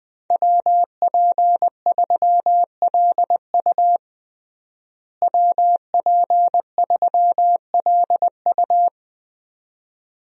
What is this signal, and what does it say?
Morse code, heard as a keyed tone, about 20 wpm, 700 Hz. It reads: WP3LU WP3LU
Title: USA Call Signs - 5 Characters - Repeated 2x - 20wpm